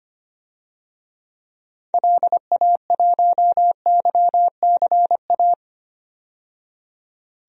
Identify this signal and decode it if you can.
Morse code, heard as a keyed tone, about 25 wpm, 700 Hz. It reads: LA1YCA
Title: DXCC-100 Call Signs - 6 Characters - 25wpm